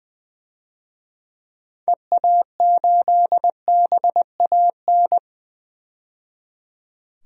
Morse code, heard as a keyed tone, about 20 wpm, 700 Hz. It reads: EA8BAN
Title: DXCC-100 Call Signs - 6 Characters - 20wpm